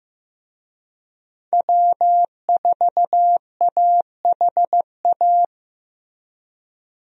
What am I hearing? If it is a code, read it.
Morse code, heard as a keyed tone, about 15 wpm, 700 Hz. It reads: W4AHA
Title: USA Call Signs - 5 Characters - Repeated 3x - 15wpm